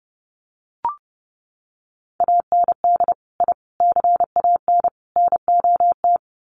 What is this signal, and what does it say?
Morse code, heard as a keyed tone, about 30 wpm, 700 Hz. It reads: AND I CAN NOT
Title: Sets of 4 Words 30wpm